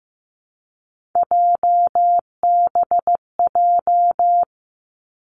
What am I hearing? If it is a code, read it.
Morse code, heard as a keyed tone, about 15 wpm, 700 Hz. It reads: JBJ
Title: Three Challenging Letter-Number Combinations 15wpm